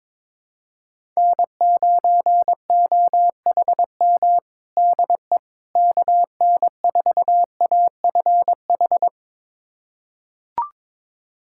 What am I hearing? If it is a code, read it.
Morse code, heard as a keyed tone, about 22 wpm, 700 Hz. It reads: N9OHM DE KN4AFH
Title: USA Callsign DE Callsign 22wpm